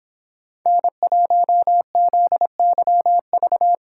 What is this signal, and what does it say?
Morse code, heard as a keyed tone, about 24 wpm, 700 Hz. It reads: N1ZYV